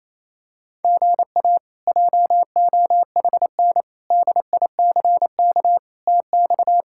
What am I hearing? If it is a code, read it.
Morse code, heard as a keyed tone, about 28 wpm, 700 Hz. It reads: GA JOHN DICK TX
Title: K1USN SST QSO Practice - 28wpm